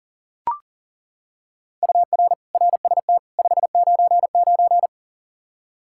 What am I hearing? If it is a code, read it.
Morse code, heard as a keyed tone, about 38 wpm, 700 Hz. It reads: UR RST 599
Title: Practice QSO - Exchange 1 - Beginning - 40wpm